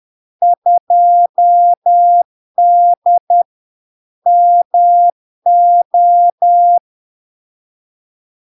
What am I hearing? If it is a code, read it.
Morse code, heard as a keyed tone, about 10 wpm, 700 Hz. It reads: 2D MO
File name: LARK-FD-2022-Log-in-CW-10wpm-700-Hz